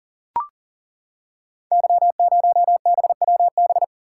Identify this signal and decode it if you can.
Morse code, heard as a keyed tone, about 39 wpm, 700 Hz. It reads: Y0BWB
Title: Code Groups - 5 Letter:Numbers - 40wpm